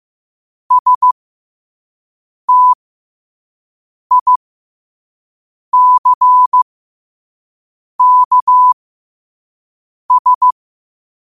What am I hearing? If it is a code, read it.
Morse code, heard as a keyed tone, about 15 wpm, spaced as for 5 wpm, 1000 Hz. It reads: STICKS
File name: qotdfiles / 5wpm